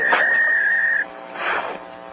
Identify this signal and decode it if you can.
Fax machine handshake.
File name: Hotel Calls / hotel fax machine violation